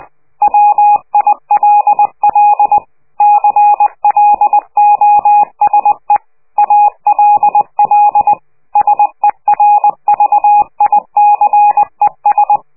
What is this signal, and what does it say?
Morse code, heard as a keyed tone, about 20 wpm, 830 Hz. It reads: WILL CLOSE ALL SERVICES
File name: worldcoastal / scheveningenradio / morsepch